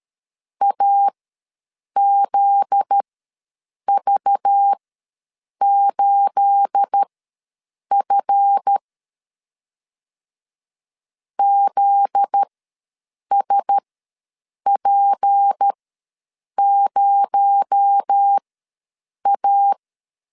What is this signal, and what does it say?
Morse code, heard as a keyed tone, about 13 wpm, spaced as for 7 wpm, 800 Hz. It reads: AZV8F ZSP0A
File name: LEZIONI / 9 28 ws 12wpm 3354